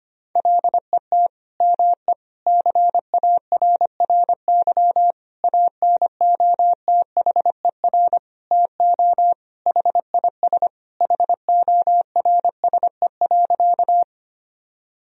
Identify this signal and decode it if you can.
Morse code, heard as a keyed tone, about 25 wpm, 700 Hz. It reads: LET ME CARRY ANOTHER TO HIS HORSE.
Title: Sentences from Top 1000 Words - Review Part 1 - 25wpm